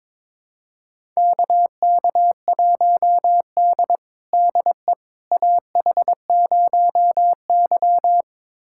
Morse code, heard as a keyed tone, about 22 wpm, 700 Hz. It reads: KK1D DE AH0Y
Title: USA Callsign DE Callsign - 4 Characters - 22wpm